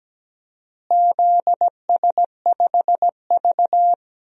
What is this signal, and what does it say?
Morse code, heard as a keyed tone, about 17 wpm, 700 Hz. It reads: ZS5V
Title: DXCC-100 Call Signs - 4 Characters - Repeated 2x - 17wpm